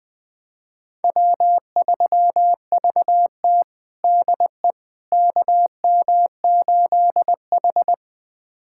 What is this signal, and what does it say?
Morse code, heard as a keyed tone, about 20 wpm, 700 Hz. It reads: W3VT DE KM8H
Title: USA Callsign DE Callsign - 4 Characters - 20wpm